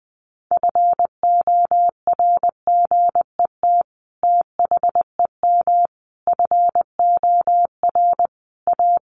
Morse code, heard as a keyed tone, about 20 wpm, 700 Hz. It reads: FORGET THEM FOR A
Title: Sentences from Top 1200 Words 20wpm